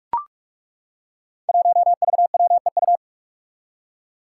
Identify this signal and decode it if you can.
Morse code, heard as a keyed tone, about 42 wpm, 700 Hz. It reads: 1VWEU
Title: Five Challenging Letter-Number Combinations 45wpm